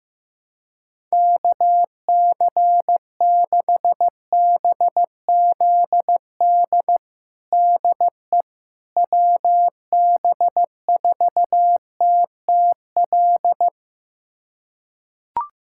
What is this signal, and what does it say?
Morse code, heard as a keyed tone, about 16 wpm, 700 Hz. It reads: KC6BZD DE WB4TTL
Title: USA Callsign DE Callsign - 6 Characters - 15wpm